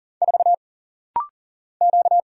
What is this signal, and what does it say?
Morse code, heard as a keyed tone, about 38 wpm, 700 Hz. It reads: 4 Q
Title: Single Letter-Number Alt - Rapid-Fire 40wpm